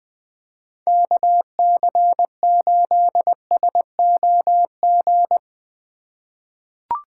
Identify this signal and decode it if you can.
Morse code, heard as a keyed tone, about 20 wpm, 700 Hz. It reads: KC8SOG
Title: USA Call Signs - 2x Character Spacing - 20wpm